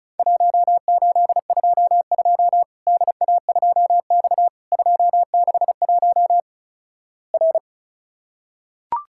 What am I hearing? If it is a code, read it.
Morse code, heard as a keyed tone, about 34 wpm, 700 Hz. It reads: 1822 DA2X 261 R
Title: Worked All Europe DX Contest - QTC Exchange - 35wpm